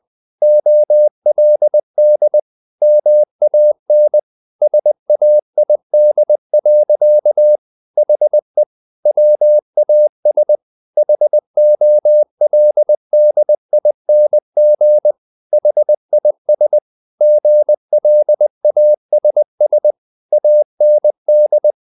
Morse code, heard as a keyed tone, about 20 wpm, 600 Hz. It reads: OLD MAN SAID. HE WAS HOLDING HIS GLASS AND
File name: the old man and the sea / the old man and the sea part 1 20wpm